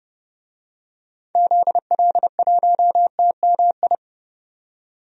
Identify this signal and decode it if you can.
Morse code, heard as a keyed tone, about 30 wpm, 700 Hz. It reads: ZL1TMI
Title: DXCC-100 Call Signs - 6 Characters - 30wpm